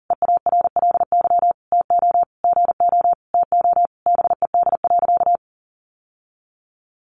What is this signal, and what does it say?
Morse code, heard as a keyed tone, about 40 wpm, 700 Hz. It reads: EARLY TO GO TO BED.
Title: Sentences from Top 700 Words 40wpm